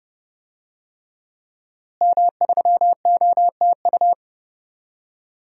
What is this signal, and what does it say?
Morse code, heard as a keyed tone, about 30 wpm, 700 Hz. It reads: M3OTU
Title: DXCC-100 Call Signs - 5 Characters - Repeated 3x - 30wpm